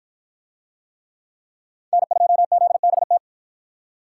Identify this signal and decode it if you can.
Morse code, heard as a keyed tone, about 50 wpm, 700 Hz. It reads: N2ZBT